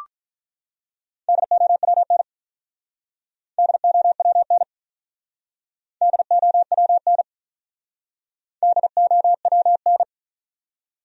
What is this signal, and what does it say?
Morse code, heard as a keyed tone, about 48 wpm, 700 Hz. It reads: DOWN DOWN DOWN DOWN